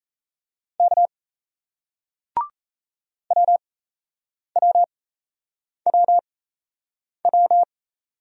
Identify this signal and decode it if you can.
Morse code, heard as a keyed tone, about 38 wpm, spaced as for 8 wpm, 700 Hz. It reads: K WWWW